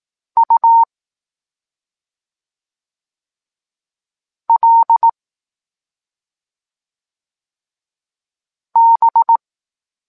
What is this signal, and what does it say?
Morse code, heard as a keyed tone, about 18 wpm, 920 Hz. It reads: U L B
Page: Tekst 23.3 – Ciąg dalszy doskonalenia odbioru poznanych liter